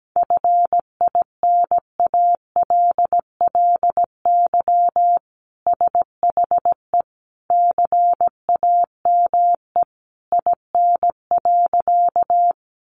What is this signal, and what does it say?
Morse code, heard as a keyed tone, about 17 wpm, 700 Hz. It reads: FINALLY SHE CAME IN.
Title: Sentences from Top 1000 Words 17wpm